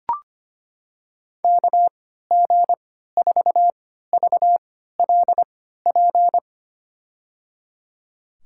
Morse code, heard as a keyed tone, about 25 wpm, spaced as for 14 wpm, 700 Hz. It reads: KG4VLP
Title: USA Call Signs - 6 Characters - 3x Character Spacing - 25wpm